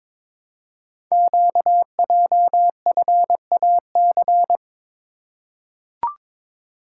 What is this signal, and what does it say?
Morse code, heard as a keyed tone, about 22 wpm, 700 Hz. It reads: QJFAC
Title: Code Groups - 5 Letters - 22wpm